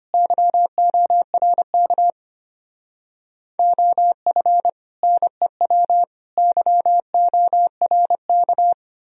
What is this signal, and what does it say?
Morse code, heard as a keyed tone, about 29 wpm, 700 Hz. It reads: YORK OF NEW YORK
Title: Speed-Racing - Sets of 3 Words 25wpm